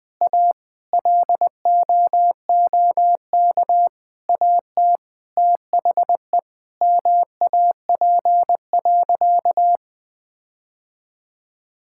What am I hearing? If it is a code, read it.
Morse code, heard as a keyed tone, about 20 wpm, 700 Hz. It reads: A LOOK AT THE MAP.
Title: Sentences with 5-Word Phrases - 20wpm